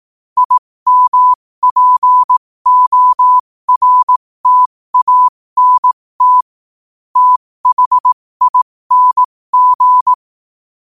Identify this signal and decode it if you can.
Morse code, heard as a keyed tone, about 18 wpm, spaced as for 15 wpm, 1000 Hz. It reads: IMPORTANT THING
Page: New quotes every day in morse code at 15 Words per minute.